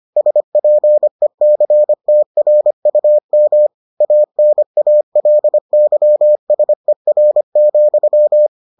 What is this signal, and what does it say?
Morse code, heard as a keyed tone, about 25 wpm, 600 Hz. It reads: SPECTRUM ANALYSER,